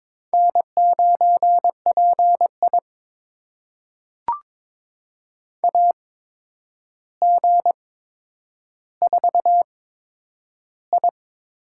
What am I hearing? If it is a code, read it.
Morse code, heard as a keyed tone, about 22 wpm, spaced as for 9 wpm, 700 Hz. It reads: N9PI AG4I
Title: USA Call Signs - 4 Characters - 8x Character Spacing - 22wpm